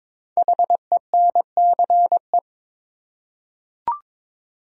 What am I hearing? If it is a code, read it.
Morse code, heard as a keyed tone, about 22 wpm, 700 Hz. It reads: HENCE
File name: 5 Letter Words - First 3 Letters Given - 22wpm